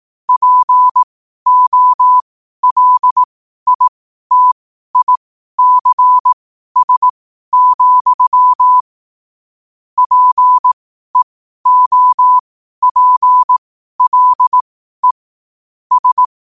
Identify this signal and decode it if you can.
Morse code, heard as a keyed tone, about 18 wpm, spaced as for 12 wpm, 1000 Hz. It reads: POLITICS, PEOPLE S
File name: 12wpm